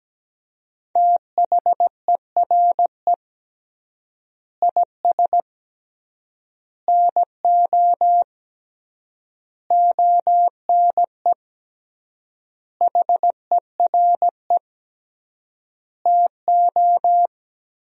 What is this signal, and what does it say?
Morse code, heard as a keyed tone, about 17 wpm, 700 Hz. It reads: THERE IS NO ONE HERE TO
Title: 3x Word-Spacing - Sentences from Top 200 Words - 17wpm